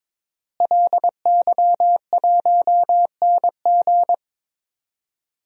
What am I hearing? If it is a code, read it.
Morse code, heard as a keyed tone, about 22 wpm, 700 Hz. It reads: LY1NG
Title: DXCC-100 Call Signs - 5 Characters - Repeated 3x - 22wpm